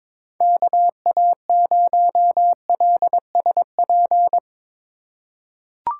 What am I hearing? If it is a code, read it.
Morse code, heard as a keyed tone, about 22 wpm, 700 Hz. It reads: KA0LSP
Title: USA Call Signs - 6 Characters - 2x Character Spacing - 22wpm